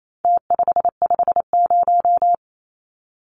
Morse code, heard as a keyed tone, about 28 wpm, 700 Hz. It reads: T550
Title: Numbers - Contest Style - 4-Digit 28wpm